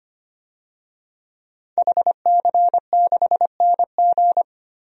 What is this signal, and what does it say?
Morse code, heard as a keyed tone, about 25 wpm, 700 Hz. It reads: HC6NG
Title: DXCC-100 Call Signs - 5 Characters - 25wpm